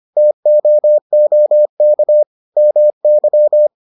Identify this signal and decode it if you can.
Morse code, heard as a keyed tone, about 25 wpm, 600 Hz. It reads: TOOK MY